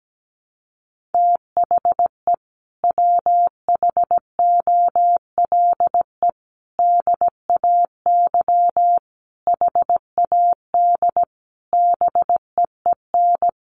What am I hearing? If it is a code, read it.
Morse code, heard as a keyed tone, about 17 wpm, 700 Hz. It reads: THE WHOLE DAY HAD BEEN
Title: Sentences from Top 500 Words 17wpm